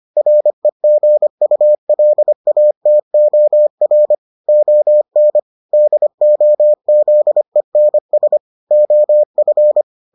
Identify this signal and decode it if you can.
Morse code, heard as a keyed tone, about 25 wpm, 600 Hz. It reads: REGULATOR ON DOZENS OF